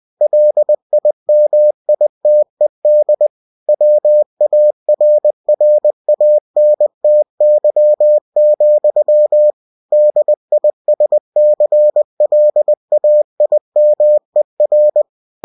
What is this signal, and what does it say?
Morse code, heard as a keyed tone, about 20 wpm, 600 Hz. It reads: LIMITED WARRANTY, DISCLAIMER